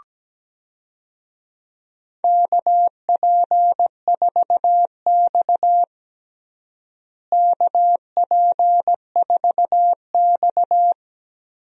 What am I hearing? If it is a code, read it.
Morse code, heard as a keyed tone, about 17 wpm, 700 Hz. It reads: KP4X KP4X
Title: DXCC-100 Call Signs - 4 Characters - Repeated 3x - 17wpm